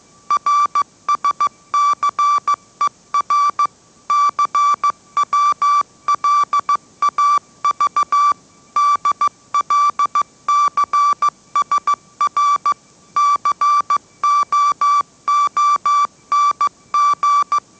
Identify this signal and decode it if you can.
Morse code, heard as a keyed tone, about 15 wpm, 1160 Hz. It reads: RSCER CWLAV DLCSR COONG
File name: mrx / mrx065 16wpm